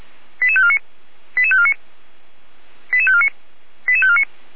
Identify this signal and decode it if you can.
Five-tone selective call (ZVEI), heard as quick runs of five tones, tones of about 70 ms. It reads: 88549 88549 88549 88549